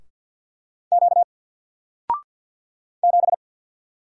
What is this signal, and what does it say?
Morse code, heard as a keyed tone, about 48 wpm, 700 Hz. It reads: Q 7
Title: Single Character (Letter:Number:Punctuation:Prosign) 50wpm